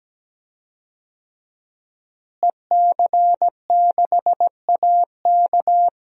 Morse code, heard as a keyed tone, about 17 wpm, 700 Hz. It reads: EC6AK
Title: DXCC-100 Call Signs - 5 Characters - Repeated 2x - 17wpm